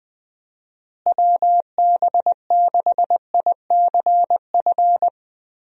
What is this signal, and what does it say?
Morse code, heard as a keyed tone, about 20 wpm, 700 Hz. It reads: WB6ICF
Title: USA Call Signs - 6 Characters - Repeated 2x - 20wpm